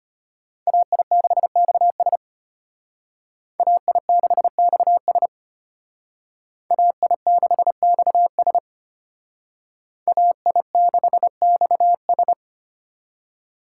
Morse code, heard as a keyed tone, about 30 wpm, 700 Hz. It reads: AI6XS AI6XS AI6XS AI6XS
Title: Speed Racing - USA 2x2 Call Signs 25wpm